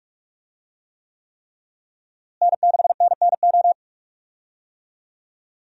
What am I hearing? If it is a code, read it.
Morse code, heard as a keyed tone, about 41 wpm, 700 Hz. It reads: N6NNO
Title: USA Call Signs - 5 Characters - Repeated 3x - 45wpm